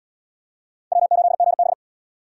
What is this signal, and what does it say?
Morse code, heard as a keyed tone, about 72 wpm, 700 Hz. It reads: W8GB